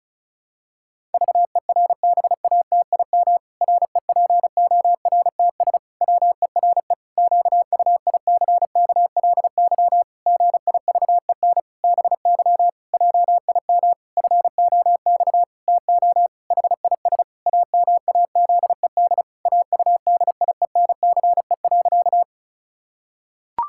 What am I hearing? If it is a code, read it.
Morse code, heard as a keyed tone, about 36 wpm, 700 Hz. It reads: VERBATIM REPORTS WERE QUICKLY GIVEN BY JIM FOX TO HIS AMAZED AUDIENCE.
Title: Pangram Challenge v2 - 35wpm